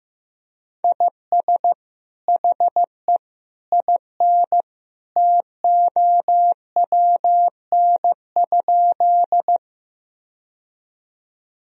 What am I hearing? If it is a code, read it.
Morse code, heard as a keyed tone, about 15 wpm, 700 Hz. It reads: IS HE IN TOWN?
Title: Sentences from Top 1000 Words - Review Part 2 - 15wpm